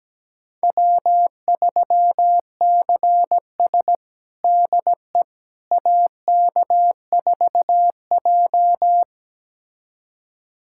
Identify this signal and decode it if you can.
Morse code, heard as a keyed tone, about 17 wpm, 700 Hz. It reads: W3CS DE AK4J
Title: USA Callsign DE Callsign - 4 Characters - 17wpm